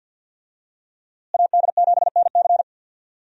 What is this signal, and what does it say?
Morse code, heard as a keyed tone, about 50 wpm, 700 Hz. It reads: AD7NC
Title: USA Call Signs - 5 Characters - Repeated 3x - 50wpm